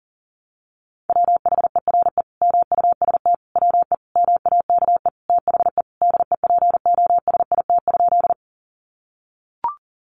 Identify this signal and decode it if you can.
Morse code, heard as a keyed tone, about 40 wpm, 700 Hz. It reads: WHERE MUST WE MAKE THE DEPOSIT?
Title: Sentences from Top 1300 Words 40wpm